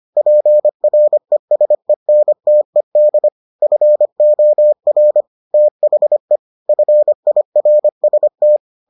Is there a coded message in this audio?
Yes: Morse code from 0.2 to 8.6 s, heard as a keyed tone, about 25 wpm, 600 Hz. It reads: PRESENTED FOR THE FIRST